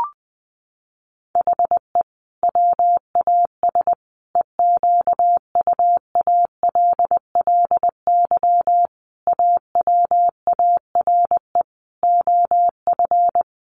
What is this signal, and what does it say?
Morse code, heard as a keyed tone, about 20 wpm, 700 Hz. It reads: HE WAS EQUALLY AWARE OF
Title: Sentences from Top 1400 Words 20wpm